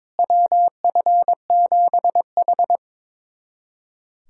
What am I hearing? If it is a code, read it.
Morse code, heard as a keyed tone, about 22 wpm, 700 Hz. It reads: WF7H
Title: USA Callsign DE Callsign - 4 Characters - 22wpm